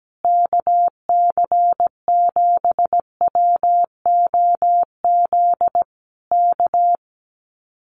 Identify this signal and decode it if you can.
Morse code, heard as a keyed tone, about 17 wpm, 700 Hz. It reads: KC7WOZ K
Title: Common QSO Phrases Encore 17wpm